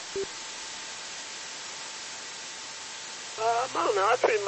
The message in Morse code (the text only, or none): none